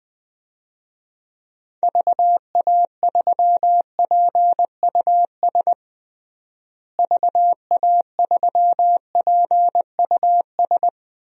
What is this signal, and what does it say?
Morse code, heard as a keyed tone, about 21 wpm, 700 Hz. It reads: VA3PUS VA3PUS
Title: DXCC-100 Call Signs - 6 Characters - Repeated 2x - 20wpm